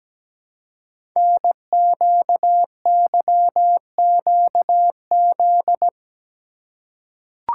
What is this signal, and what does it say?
Morse code, heard as a keyed tone, about 17 wpm, 700 Hz. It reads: NQYQZ
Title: Code Groups - 5 Letters - 17wpm